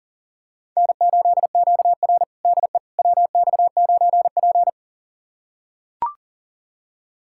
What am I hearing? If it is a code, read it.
Morse code, heard as a keyed tone, about 37 wpm, 690 Hz. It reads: N8QR DE WX9P
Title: USA Callsign DE Callsign - 4 Characters - 40wpm